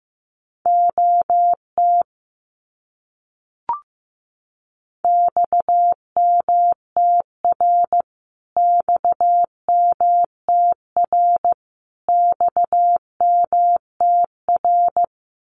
Morse code, heard as a keyed tone, about 15 wpm, 700 Hz. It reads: OT XMTR XMTR XMTR
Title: Abbreviations Repeated 3x - 15wpm